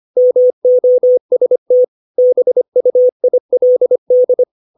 Morse code, heard as a keyed tone, about 25 wpm, 500 Hz. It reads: MOST BUILD
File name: 632 words / 632 Words 25WPM